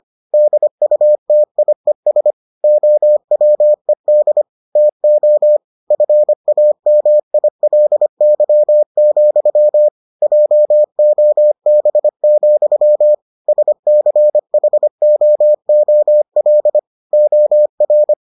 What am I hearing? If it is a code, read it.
Morse code, heard as a keyed tone, about 25 wpm, 600 Hz. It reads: DUTIES OWED TO FAMILY, JOB, SCHOOL OR